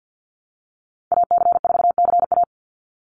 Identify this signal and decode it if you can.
Morse code, heard as a keyed tone, about 49 wpm, 700 Hz. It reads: AC4CA
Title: Call Signs Repeated 3x - 50wpm